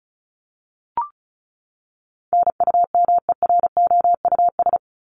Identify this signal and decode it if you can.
Morse code, heard as a keyed tone, about 34 wpm, 700 Hz. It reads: NUMEROUS
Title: Top 1251-1500 Words 35wpm